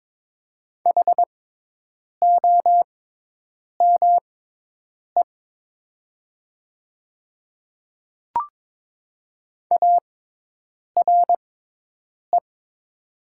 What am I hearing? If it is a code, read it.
Morse code, heard as a keyed tone, about 22 wpm, spaced as for 8 wpm, 700 Hz. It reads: HOME ARE
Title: 6x Character Spacing - Top 200 Words - 22wpm